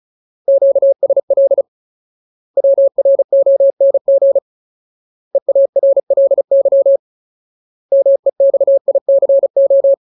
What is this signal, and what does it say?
Morse code, heard as a keyed tone, about 34 wpm, 550 Hz. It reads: QSL WRONG EARLY MEXICO
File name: Ultimate Word List / 35WPM Ultimate